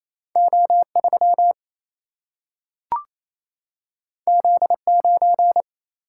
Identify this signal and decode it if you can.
Morse code, heard as a keyed tone, about 28 wpm, 700 Hz. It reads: O 3 Z 9
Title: Single-Letter Single-Number 28wpm